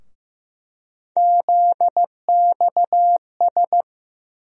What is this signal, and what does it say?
Morse code, heard as a keyed tone, about 15 wpm, 700 Hz. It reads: ZXS
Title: USA Call Sign Postfixes - 3 Letters 15wpm